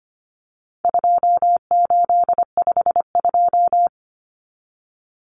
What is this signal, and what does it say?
Morse code, heard as a keyed tone, about 25 wpm, 700 Hz. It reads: 2852
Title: Common Cut Numbers - 4-Digit 25wpm